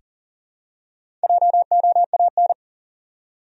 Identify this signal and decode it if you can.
Morse code, heard as a keyed tone, about 40 wpm, 700 Hz. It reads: JOAN
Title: USA Common Names - Max 5 Letters - 40wpm